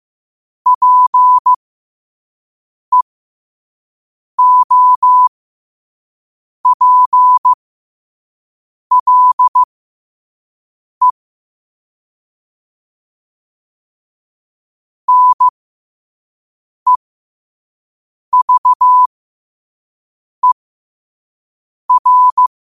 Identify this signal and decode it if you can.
Morse code, heard as a keyed tone, about 15 wpm, spaced as for 5 wpm, 1000 Hz. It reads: PEOPLE NEVER
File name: qotdfiles / 5wpm